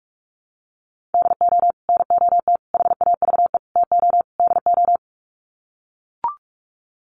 Morse code, heard as a keyed tone, about 44 wpm, 700 Hz. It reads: DO NOT HAVE TO DO
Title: Sets of 5 Words 45wpm